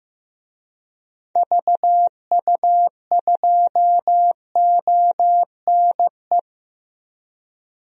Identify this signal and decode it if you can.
Morse code, heard as a keyed tone, about 15 wpm, 700 Hz. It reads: VU2ONE
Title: DXCC-100 Call Signs - 6 Characters - 15wpm